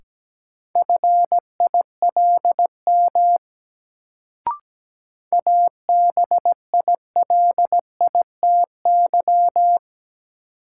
Morse code, heard as a keyed tone, about 17 wpm, 700 Hz. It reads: FILM ABILITY
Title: Sentences from Top 1000 Words - Review Part 3 - New Word Drill - 17wpm